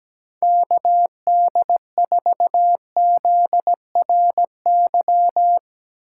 Morse code, heard as a keyed tone, about 17 wpm, 700 Hz. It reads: KD4ZRY
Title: USA Call Signs - 8x Character Spacing - 17wpm